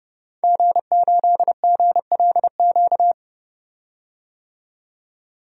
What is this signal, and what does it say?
Morse code, heard as a keyed tone, about 32 wpm, 700 Hz. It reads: G8GLQ
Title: DXCC-100 Call Signs - 5 Characters - 30wpm